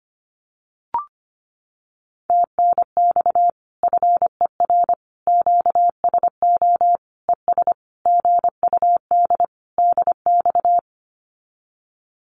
Morse code, heard as a keyed tone, about 25 wpm, 700 Hz. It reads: TNX FER QSO ES GUD DX
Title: Common QSO Phrases Encore 25wpm